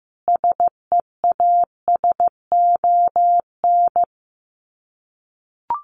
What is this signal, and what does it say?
Morse code, heard as a keyed tone, about 15 wpm, 700 Hz. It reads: SEASON
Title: Sets of 5 Words Encore 15wpm